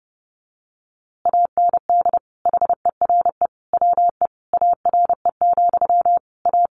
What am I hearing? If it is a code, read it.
Morse code, heard as a keyed tone, about 30 wpm, 700 Hz. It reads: AND HERE WE ARE, A
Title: A Sound Of Thunder 30wpm